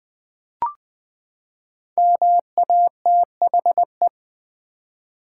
Morse code, heard as a keyed tone, about 20 wpm, 700 Hz. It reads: MATHE
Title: Build A Word - 5 Letters At A Time - 12-Letter Words - 20wpm